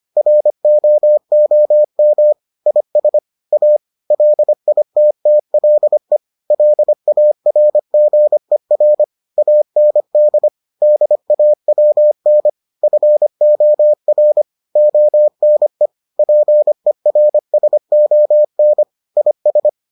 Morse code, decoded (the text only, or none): ROOM IS A LITTLE LARGER AND DAWN FOR ONE PERSON IS